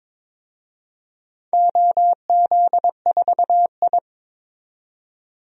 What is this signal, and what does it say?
Morse code, heard as a keyed tone, about 22 wpm, 700 Hz. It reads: OZ4I
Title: DXCC-100 Call Signs - 4 Characters - 22wpm